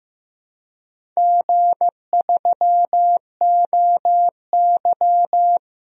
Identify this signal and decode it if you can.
Morse code, heard as a keyed tone, about 15 wpm, 700 Hz. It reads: G3OY
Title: DXCC-100 Call Signs - 4 Characters - 15wpm